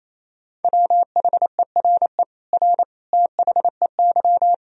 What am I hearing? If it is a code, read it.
Morse code, heard as a keyed tone, about 28 wpm, 700 Hz. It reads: WHERE R THEY
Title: Sentences from Top 200 Words and Abbreviations 28wpm